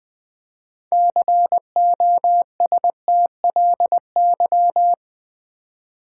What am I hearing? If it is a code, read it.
Morse code, heard as a keyed tone, about 20 wpm, 700 Hz. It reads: COSTLY
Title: Build A Word - 3 Letters At A Time - 6-Letter Words - 20wpm